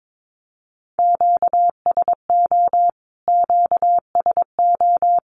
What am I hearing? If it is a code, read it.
Morse code, heard as a keyed tone, about 22 wpm, 700 Hz. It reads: QSO QSO
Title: Q-Codes Repeated 3x 22wpm